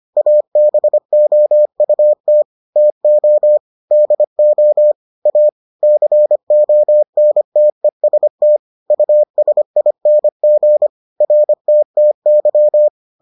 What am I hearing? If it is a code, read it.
Morse code, heard as a keyed tone, about 25 wpm, 600 Hz. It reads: ABOUT TO DO A CONTEST USING RTTY